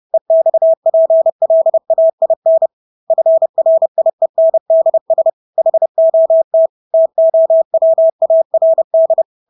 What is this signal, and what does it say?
Morse code, heard as a keyed tone, about 30 wpm, 650 Hz. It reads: EXPLAIN FRIENDS HOT TOWARD
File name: Engelske ord 30-30 wpm